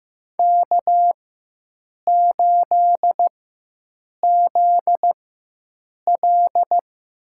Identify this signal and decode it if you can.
Morse code, heard as a keyed tone, about 15 wpm, spaced as for 7 wpm, 700 Hz. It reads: K8ZL
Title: USA Call Signs - 4 Characters - 4x Character Spacing - 15wpm